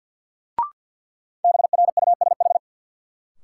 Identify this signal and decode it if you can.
Morse code, heard as a keyed tone, about 53 wpm, 700 Hz. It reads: BRUSH
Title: Sentences from Top 1100 Words - New Word Drill Alt - 50wpm